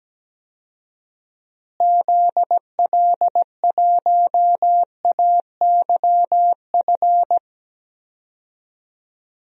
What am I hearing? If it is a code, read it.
Morse code, heard as a keyed tone, about 17 wpm, 700 Hz. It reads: ZL1AYF
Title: DXCC-100 Call Signs - 6 Characters - Repeated 3x - 17wpm